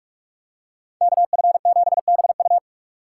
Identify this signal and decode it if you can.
Morse code, heard as a keyed tone, about 44 wpm, 700 Hz. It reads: KF7BU